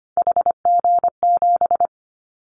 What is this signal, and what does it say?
Morse code, heard as a keyed tone, about 25 wpm, 700 Hz. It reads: HG7
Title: Callsign Prefixes for Top 25 DXCC - Repeated 3x - 25wpm